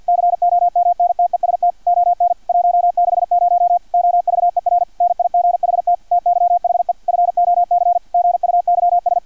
Morse code, heard as a keyed tone, about 50 wpm, 700 Hz, with radio noise. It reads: COMNTEST ON 160 OVER NIGHT TYHE FOC GUYS
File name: archived / CRSnet